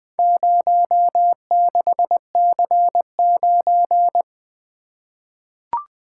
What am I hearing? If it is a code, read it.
Morse code, heard as a keyed tone, about 21 wpm, 700 Hz. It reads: 06C9
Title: Four Letter-Number Combinations - 20wpm